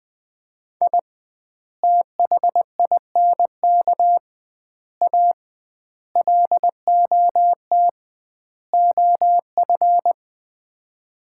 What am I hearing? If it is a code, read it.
Morse code, heard as a keyed tone, about 20 wpm, 700 Hz. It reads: I THINK A LOT OF
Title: 2x Word-Spacing - Sets of 5 Words - 20wpm